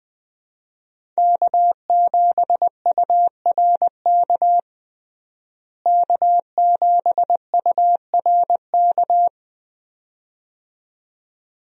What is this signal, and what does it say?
Morse code, heard as a keyed tone, about 20 wpm, 700 Hz. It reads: K7URK K7URK
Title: USA Call Signs - 5 Characters - Repeated 2x - 20wpm